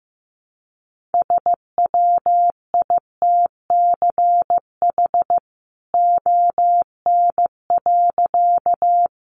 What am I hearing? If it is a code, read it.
Morse code, heard as a keyed tone, about 15 wpm, 700 Hz. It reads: SWITCH ON.
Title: A Sound Of Thunder 15wpm